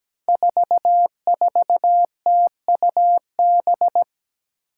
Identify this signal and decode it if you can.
Morse code, heard as a keyed tone, about 17 wpm, 700 Hz. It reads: 44TUB
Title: Five Challenging Letter-Number Combinations 17wpm